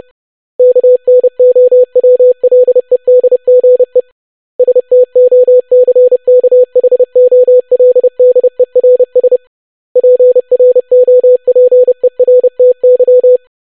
Morse code, heard as a keyed tone, about 30 wpm, 500 Hz. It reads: KNOWLEDGE STOCKHOLDERS PROPERTY
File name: longwords-25wpm